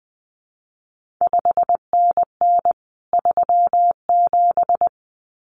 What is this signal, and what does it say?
Morse code, heard as a keyed tone, about 20 wpm, 700 Hz. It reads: 5NN 37
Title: CQ WW DX Contest Exchange Encore - 20wpm